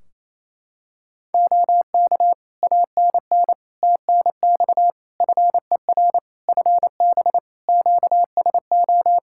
Morse code, heard as a keyed tone, about 28 wpm, 700 Hz. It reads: OK ANN TNX FER FB QSO
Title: Practice QSO - Exchange 4 - Ending - 28wpm